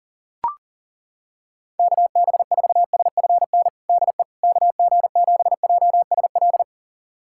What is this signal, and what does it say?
Morse code, heard as a keyed tone, about 40 wpm, 700 Hz. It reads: KB4SFN DE KG7JSL
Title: USA Callsign DE Callsign - 6 Characters - 40wpm